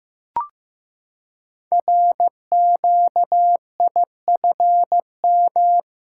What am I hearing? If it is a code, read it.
Morse code, heard as a keyed tone, about 15 wpm, 700 Hz. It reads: RQIFM
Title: Code Groups - 5 Letters - 15wpm